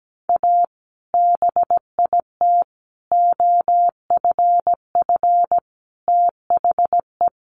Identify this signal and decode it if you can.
Morse code, heard as a keyed tone, about 17 wpm, 700 Hz. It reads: A BIT OFF THE
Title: Sentences from Top 900 Words 17wpm